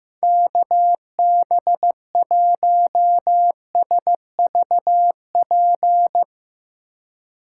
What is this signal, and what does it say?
Morse code, heard as a keyed tone, about 15 wpm, 700 Hz. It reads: KB1SVP
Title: USA Call Signs - 6 Characters - 2x Character Spacing - 15wpm